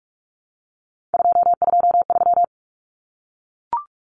Morse code, heard as a keyed tone, about 44 wpm, 700 Hz. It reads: 223
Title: Common Cut Numbers - 3-Digit 45wpm